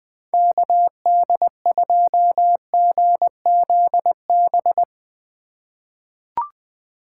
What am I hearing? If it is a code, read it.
Morse code, heard as a keyed tone, about 20 wpm, 700 Hz. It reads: KD2GZB
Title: USA Call Signs - 6 Characters - 2x Character Spacing - 20wpm